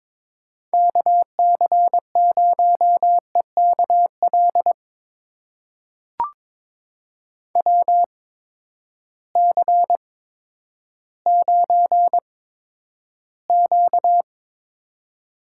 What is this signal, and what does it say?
Morse code, heard as a keyed tone, about 22 wpm, spaced as for 11 wpm, 700 Hz. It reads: KC0EKL WC9Q
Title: USA Call Signs - 8x Character Spacing - 22wpm